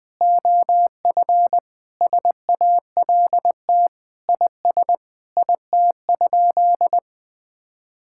Morse code, heard as a keyed tone, about 20 wpm, 700 Hz. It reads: OF SALT IS IT?
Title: Sentences from Top 1000 Words - Review Part 5 - 20wpm